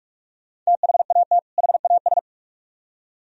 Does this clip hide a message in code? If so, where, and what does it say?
Yes: Morse code from 0.7 to 2.2 s, heard as a keyed tone, about 44 wpm, 700 Hz. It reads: THAT HAS